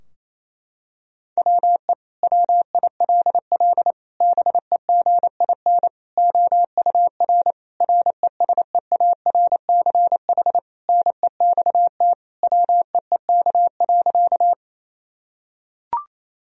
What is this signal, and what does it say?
Morse code, heard as a keyed tone, about 28 wpm, 700 Hz. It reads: WE WILL BEGIN OUR RESEARCH NEXT WEEK.
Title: Sentences from Top 1000 Words - Review Part 4 - 28wpm